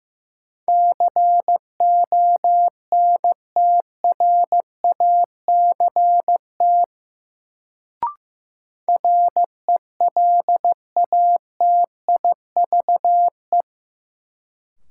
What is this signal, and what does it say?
Morse code, heard as a keyed tone, about 16 wpm, 700 Hz. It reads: CONTRACT RELATIVE
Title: Sentences from Top 1400 Words - New Word Drill Alt - 15wpm